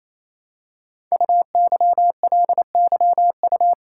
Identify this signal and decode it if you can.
Morse code, heard as a keyed tone, about 28 wpm, 700 Hz. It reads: UYLYU
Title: Code Groups - 5 Letters - 28wpm